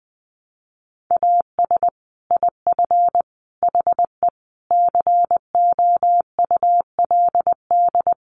Morse code, heard as a keyed tone, about 20 wpm, 700 Hz. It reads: AS IF HE COULD
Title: Sentences from Top 1300 Words 20wpm